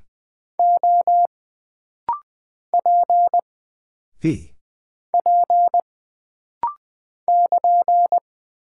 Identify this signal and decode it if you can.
Morse code, heard as a keyed tone, about 20 wpm, 700 Hz. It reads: O P P <KN>
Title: Single Character (Letter:Number:Punctuation:Prosign) - Rapid-Fire - 20wpm